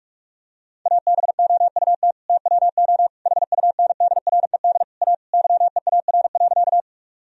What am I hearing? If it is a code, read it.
Morse code, heard as a keyed tone, about 46 wpm, 700 Hz. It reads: ABOUT TWO HUNDRED A YEAR.
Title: Sentences from Top 1000 Words - Review Part 2 - 45wpm